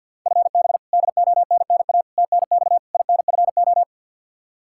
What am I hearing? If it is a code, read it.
Morse code, heard as a keyed tone, about 53 wpm, 700 Hz. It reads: FB DONNA TNX INFO
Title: Practice QSO - Exchange 3 - More Info - 50wpm